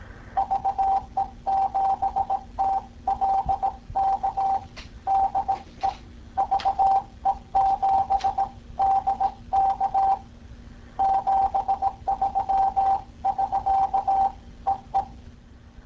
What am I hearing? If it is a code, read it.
Morse code, heard as a keyed tone, about 17 wpm, 780 Hz. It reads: VE7TLK DE VE7DK 73 <SK> EE